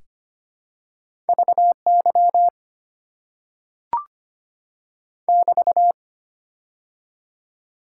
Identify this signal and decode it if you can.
Morse code, heard as a keyed tone, about 25 wpm, 690 Hz. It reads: VY <BT>
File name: Top 90 QSO Elements - 6x Character Spacing - 25wpm